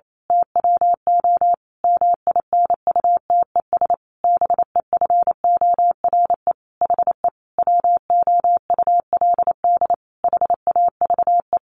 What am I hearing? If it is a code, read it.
Morse code, heard as a keyed tone, about 28 wpm, 700 Hz. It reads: TWO MINUTES BEFORE HE WOULD HAVE
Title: A Sound Of Thunder 28wpm